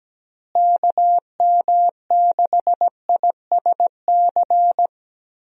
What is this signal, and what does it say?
Morse code, heard as a keyed tone, about 17 wpm, 700 Hz. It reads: KM6ISC
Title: USA Call Signs - 6 Characters - Repeated 3x - 17wpm